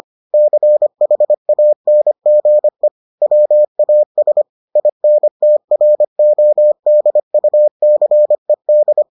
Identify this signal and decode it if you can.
Morse code, heard as a keyed tone, about 25 wpm, 600 Hz. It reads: CHANGE WAS INTRODUCED